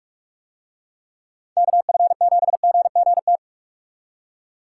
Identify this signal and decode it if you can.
Morse code, heard as a keyed tone, about 40 wpm, 700 Hz. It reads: KF7GGT